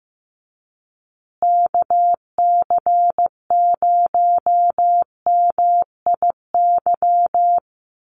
Morse code, heard as a keyed tone, about 15 wpm, 700 Hz. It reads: KC0MIY
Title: USA Call Signs Repeated 3x 15wpm